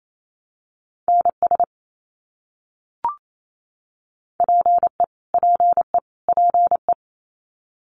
Morse code, heard as a keyed tone, about 28 wpm, 700 Hz. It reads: NS PE PE PE
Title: Canadian Provinces and Territories - Repeated 3x - 28wpm